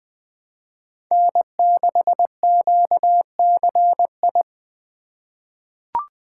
Morse code, heard as a keyed tone, about 20 wpm, 700 Hz. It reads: N6QCI
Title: USA Call Signs - 5 Characters - 2x Character Spacing - 20wpm